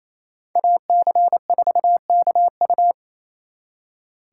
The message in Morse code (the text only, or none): AC4KU